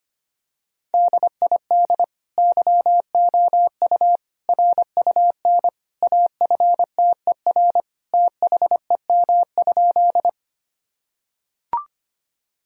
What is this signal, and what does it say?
Morse code, heard as a keyed tone, about 25 wpm, 700 Hz. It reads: DID YOU RUN AFTER THEM?
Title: Sentences from Top 1000 Words - Review Part 1 - 25wpm